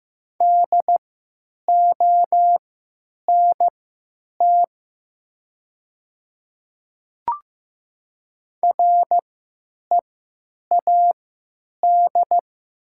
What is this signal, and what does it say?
Morse code, heard as a keyed tone, about 15 wpm, spaced as for 9 wpm, 700 Hz. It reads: DONT READ
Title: 3x Character Spacing - Top 300 Words - 15wpm